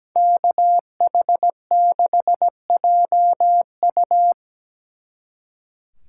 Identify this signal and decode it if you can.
Morse code, heard as a keyed tone, about 20 wpm, 700 Hz. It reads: KH6JU
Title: Speed Racing - USA 2x2 Call Signs 17wpm